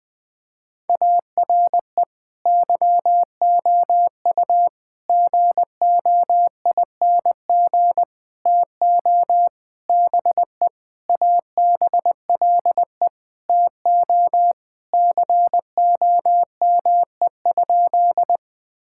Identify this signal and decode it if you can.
Morse code, heard as a keyed tone, about 20 wpm, 700 Hz. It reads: ARE YOU GOING TO BE ABLE TO COME?
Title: Sentences with 5-Word Phrases - Spoken Beginning - 20wpm